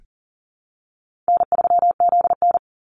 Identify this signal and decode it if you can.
Morse code, heard as a keyed tone, about 40 wpm, 700 Hz. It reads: N3ZN
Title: Call Signs - 40wpm